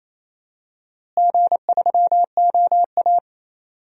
Morse code, heard as a keyed tone, about 28 wpm, 700 Hz. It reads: G3OA
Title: DXCC-100 Call Signs - 4 Characters - Repeated 2x - 28wpm